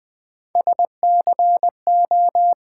Morse code, heard as a keyed tone, about 20 wpm, 700 Hz. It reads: SCO
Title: Build A Word - 1 Letter At A Time - 3 to 5-Letter Words - 20wpm